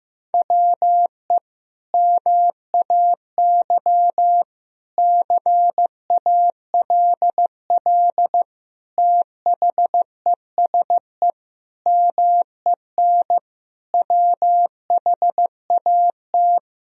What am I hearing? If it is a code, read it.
Morse code, heard as a keyed tone, about 15 wpm, 700 Hz. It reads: WE MAY CALL THESE MEN WHAT
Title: Sentences from Top 1000 Words - Review Part 1 - 15wpm